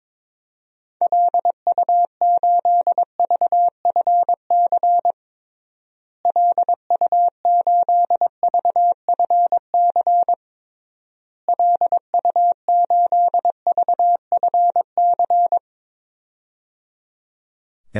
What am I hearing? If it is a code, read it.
Morse code, heard as a keyed tone, about 22 wpm, 700 Hz. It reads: LU8VFC LU8VFC LU8VFC
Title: DXCC-100 Call Signs - 6 Characters - Repeated 3x - 22wpm